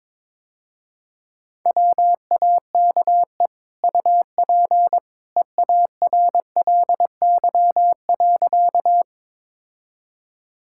Morse code, heard as a keyed tone, about 21 wpm, 700 Hz. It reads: WAKE UP EARLY.
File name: practice / 22wpm / Easy Sentences with Hints - 22wpm